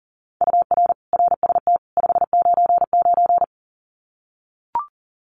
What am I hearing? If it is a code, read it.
Morse code, heard as a keyed tone, about 37 wpm, 700 Hz. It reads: UR RST 599
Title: Common QSO Phrases 40wpm